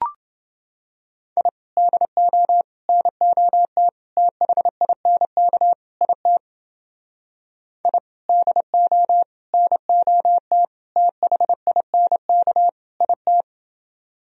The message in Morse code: I DO NOT THINK IT I DO NOT THINK IT

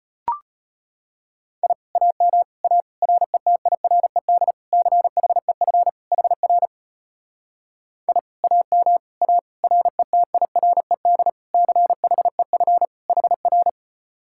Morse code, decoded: I AM A RETIRED CHEF HR I AM A RETIRED CHEF HR

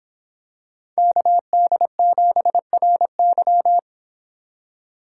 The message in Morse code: KD7RY

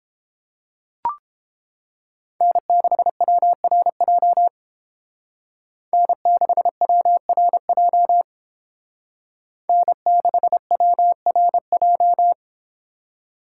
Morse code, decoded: N6WRJ N6WRJ N6WRJ

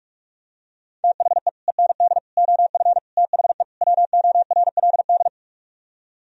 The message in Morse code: THE END OF THE WORLD

